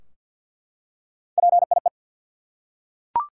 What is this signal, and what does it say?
Morse code, heard as a keyed tone, about 48 wpm, 700 Hz. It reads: PIE
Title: Three Letter Words - 50wpm